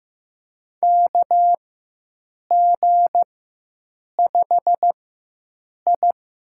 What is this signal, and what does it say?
Morse code, heard as a keyed tone, about 15 wpm, spaced as for 7 wpm, 700 Hz. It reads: KG5I
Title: USA Call Signs - 4x Character Spacing - 15wpm